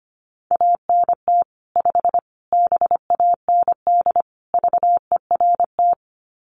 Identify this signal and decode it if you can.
Morse code, heard as a keyed tone, about 25 wpm, 700 Hz. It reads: ANT 5 BAND VERT
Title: Common QSO Phrases 25wpm